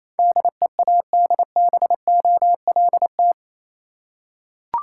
Morse code, heard as a keyed tone, about 27 wpm, 690 Hz. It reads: DEADBOLT
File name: Easy Words - Speed-Racing with Hints - 22wpm